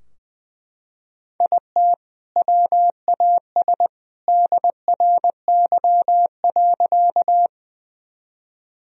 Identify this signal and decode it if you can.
Morse code, heard as a keyed tone, about 20 wpm, 700 Hz. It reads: IT WAS DRY.
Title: Sentences from Top 1000 Words - Review Part 3 - 20wpm